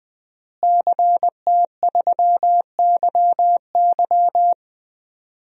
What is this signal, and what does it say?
Morse code, heard as a keyed tone, about 20 wpm, 700 Hz. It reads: CT3YY
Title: DXCC-100 Call Signs - 5 Characters - Repeated 3x - 20wpm